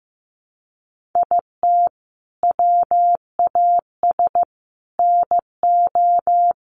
Morse code, heard as a keyed tone, about 14 wpm, 700 Hz. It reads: IT WAS NO
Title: Sentences from Top 1500 Words 15wpm